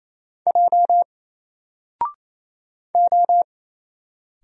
Morse code, heard as a keyed tone, about 27 wpm, 700 Hz. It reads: J O
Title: Single Letter-Number Alt 28wpm